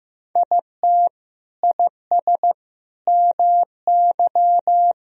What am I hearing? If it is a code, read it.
Morse code, heard as a keyed tone, about 15 wpm, 700 Hz. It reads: IT IS MY
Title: Sentences from Top 1000 Words - Review Part 2 - 15wpm